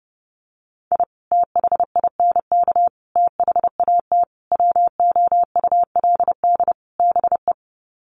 Morse code, heard as a keyed tone, about 30 wpm, 700 Hz. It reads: I THINK THAT WOULD BE
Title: Sets of 5 Words Encore2 30wpm